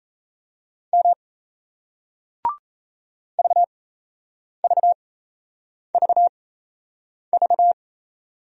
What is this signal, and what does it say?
Morse code, heard as a keyed tone, about 40 wpm, spaced as for 8 wpm, 700 Hz. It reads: M VVVV